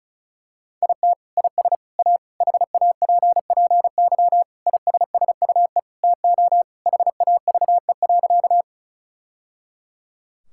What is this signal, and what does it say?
Morse code, heard as a keyed tone, about 34 wpm, 700 Hz. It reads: IT IS A HAPPY ISSUE TO HAVE.
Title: Sentences from Top 1000 Words - Review Part 6 - 35wpm